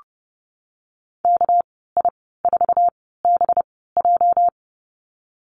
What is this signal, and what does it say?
Morse code, heard as a keyed tone, about 30 wpm, spaced as for 17 wpm, 700 Hz. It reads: KI4BJ
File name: Lesson 37g - Letter x - Call Signs 30wpm